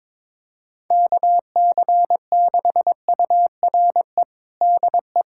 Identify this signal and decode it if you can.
Morse code, heard as a keyed tone, about 22 wpm, 700 Hz. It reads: KC6URE DE
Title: USA Callsign DE Callsign - 6 Characters - 22wpm